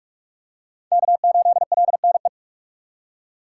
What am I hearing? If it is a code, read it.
Morse code, heard as a keyed tone, about 43 wpm, 700 Hz. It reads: K8LNE
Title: USA Call Signs - 5 Characters - 2x Character Spacing - 45wpm